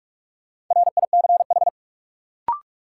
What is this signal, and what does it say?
Morse code, heard as a keyed tone, about 41 wpm, 700 Hz. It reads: RICH
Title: Sentences from Top 1100 Words - New Word Drill Alt - 45wpm